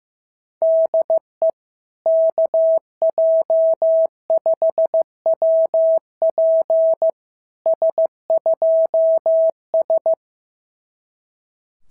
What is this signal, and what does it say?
Morse code, heard as a keyed tone, about 14 wpm, 650 Hz. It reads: DE KJ5WP S2S
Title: SOTA QSO Practice 15wpm